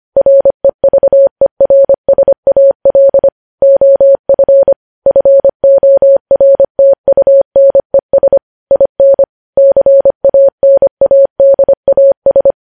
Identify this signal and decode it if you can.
Morse code, heard as a keyed tone, about 25 wpm, 550 Hz. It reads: REVERSAL OF FORTUNES IN CANADAS
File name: podcast / News headlines-1x-25x25